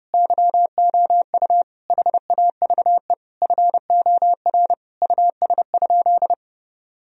Morse code, heard as a keyed tone, about 31 wpm, 700 Hz. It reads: YOU HAVE FOR US?
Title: Sentences from Top 1000 Words - Review Part 3 - 30wpm